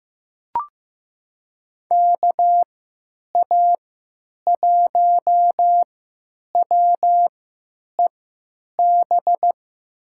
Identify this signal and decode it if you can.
Morse code, heard as a keyed tone, about 15 wpm, spaced as for 9 wpm, 700 Hz. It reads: KA1WEB
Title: USA Call Signs - 3x Character Spacing - 15wpm